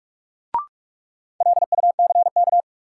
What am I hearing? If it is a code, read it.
Morse code, heard as a keyed tone, about 46 wpm, 700 Hz. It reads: LUCK
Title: Sentences from Top 1500 Words - New Word Drill - 45wpm